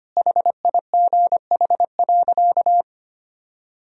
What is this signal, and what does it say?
Morse code, heard as a keyed tone, about 25 wpm, 700 Hz. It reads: HIGH.
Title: Sentences from Top 1000 Words - Review Part 4 - 25wpm